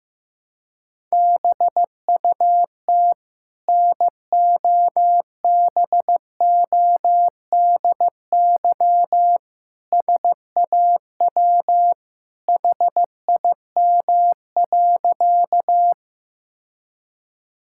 Morse code, heard as a keyed tone, about 15 wpm, 700 Hz. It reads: BUT NOBODY SAW HIM.
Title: Sentences from Top 1600 Words 15wpm